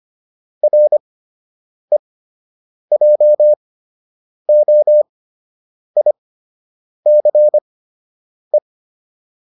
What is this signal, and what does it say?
Morse code, heard as a keyed tone, about 25 wpm, spaced as for 8 wpm, 600 Hz. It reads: REJOICE